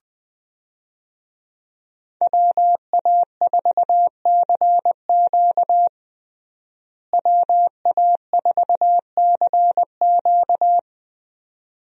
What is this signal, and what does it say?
Morse code, heard as a keyed tone, about 21 wpm, 700 Hz. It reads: WA4CQ WA4CQ
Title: USA Call Signs - 5 Characters - Repeated 2x - 20wpm